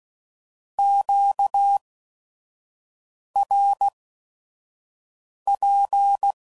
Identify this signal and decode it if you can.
Morse code, heard as a keyed tone, about 16 wpm, spaced as for 5 wpm, 800 Hz. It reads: QRP
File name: CW Code Groups @5WPM / Q Calls / Q Calls Twice